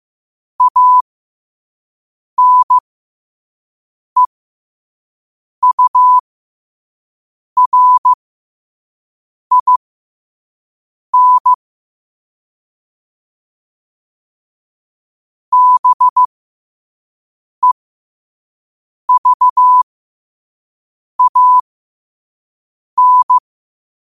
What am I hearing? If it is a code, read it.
Morse code, heard as a keyed tone, about 15 wpm, spaced as for 5 wpm, 1000 Hz. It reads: ANEURIN BEVAN
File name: qotdfiles / 5wpm